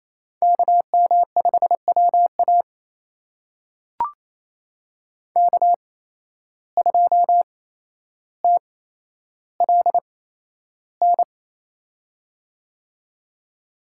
Morse code, heard as a keyed tone, about 28 wpm, spaced as for 12 wpm, 700 Hz. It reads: KM5WA K2TLN
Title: USA Call Signs - 8x Character Spacing - 28wpm